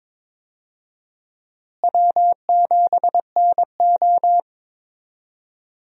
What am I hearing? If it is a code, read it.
Morse code, heard as a keyed tone, about 22 wpm, 700 Hz. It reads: W7NO
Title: USA Call Signs - 4 Characters - Repeated 2x - 22wpm